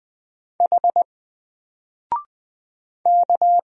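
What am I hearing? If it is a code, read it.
Morse code, heard as a keyed tone, about 20 wpm, 700 Hz. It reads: H K
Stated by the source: Single letter and number Morse drill at 20 wpm, from N4LCD.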